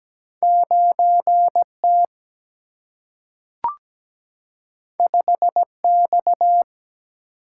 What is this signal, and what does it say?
Morse code, heard as a keyed tone, about 17 wpm, 700 Hz. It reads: 9 T 5 X
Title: Single-Letter Single-Number 17wpm